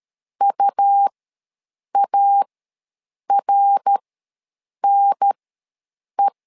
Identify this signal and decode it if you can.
Morse code, heard as a keyed tone, about 13 wpm, spaced as for 7 wpm, 800 Hz. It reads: UARNE
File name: LEZIONI / 3 MWRO / 3 MWRO ws 12wpm 1914